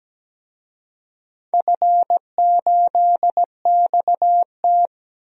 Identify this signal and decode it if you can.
Morse code, heard as a keyed tone, about 17 wpm, 700 Hz. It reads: F8XT
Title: DXCC-100 Call Signs - 4 Characters - 17wpm